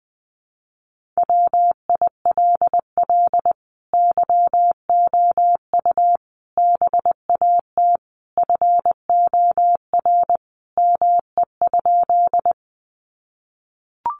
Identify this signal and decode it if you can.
Morse code, heard as a keyed tone, about 20 wpm, 700 Hz. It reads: WILL YOU BAT FOR ME?
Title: Sentences from Top 1500 Words 20wpm